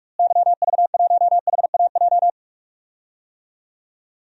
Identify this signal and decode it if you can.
Morse code, heard as a keyed tone, about 46 wpm, 700 Hz. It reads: YV1HAJ
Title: DXCC-100 Call Signs - 6 Characters - Repeated 2x - 45wpm